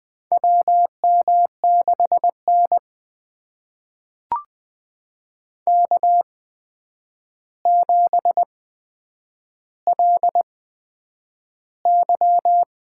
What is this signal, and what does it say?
Morse code, heard as a keyed tone, about 20 wpm, spaced as for 9 wpm, 700 Hz. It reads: WM6N K7LY
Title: USA Call Signs - 4 Characters - 8x Character Spacing - 20wpm